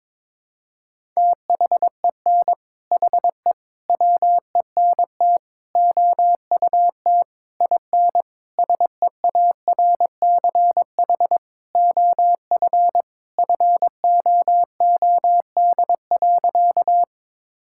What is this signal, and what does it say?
Morse code, heard as a keyed tone, about 22 wpm, 700 Hz. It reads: THEN HE WENT OUT IN SEARCH OF FOOD.
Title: Sentences from Top 1000 Words - Review Part 5 - 22wpm